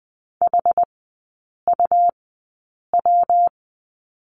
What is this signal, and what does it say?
Morse code, heard as a keyed tone, about 20 wpm, spaced as for 8 wpm, 700 Hz. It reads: HUW
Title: 114 Lesson 19e - Letter w - Three Characters 20wpm